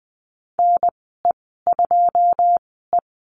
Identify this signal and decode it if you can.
Morse code, heard as a keyed tone, about 20 wpm, spaced as for 15 wpm, 700 Hz. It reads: NE2E
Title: 088 Lesson 15g - Number 5 - Call Signs 20wpm